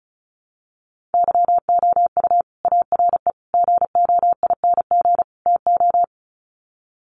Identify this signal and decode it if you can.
Morse code, heard as a keyed tone, about 34 wpm, 700 Hz. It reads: YOU ARE GOING TO
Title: Sets of 4 Words 35wpm